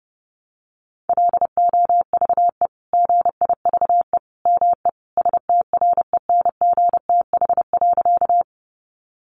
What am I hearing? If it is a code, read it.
Morse code, heard as a keyed tone, about 30 wpm, 700 Hz. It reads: LOVE GIVE ME STRENGTH.
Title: Sentences from Top 1300 Words 30wpm